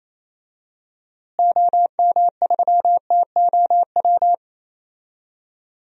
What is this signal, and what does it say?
Morse code, heard as a keyed tone, about 28 wpm, 700 Hz. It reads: OM3TOW
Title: DXCC-100 Call Signs - 6 Characters - Repeated 2x - 28wpm